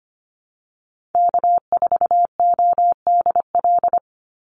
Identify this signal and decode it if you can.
Morse code, heard as a keyed tone, about 25 wpm, 700 Hz. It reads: K4ODL
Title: Call Signs Repeated 2x - 25wpm